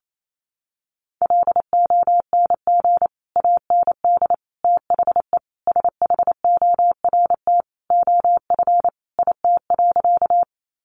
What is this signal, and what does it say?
Morse code, heard as a keyed tone, about 28 wpm, 700 Hz. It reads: LONG AND THE SHORT OF IT.
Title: Sentences from Top 400 Words 28wpm